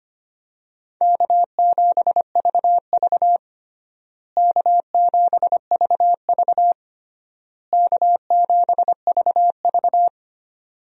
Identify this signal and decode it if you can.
Morse code, heard as a keyed tone, about 25 wpm, 700 Hz. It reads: K7VV K7VV K7VV
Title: USA Call Signs - 4 Characters - Repeated 3x - 25wpm